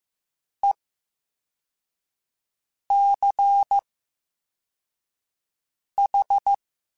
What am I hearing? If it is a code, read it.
Morse code, heard as a keyed tone, about 15 wpm, 800 Hz. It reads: E C H
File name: Special INGs-mp3s / Special INGs-mp3s / VerySlowToFaster Ing @13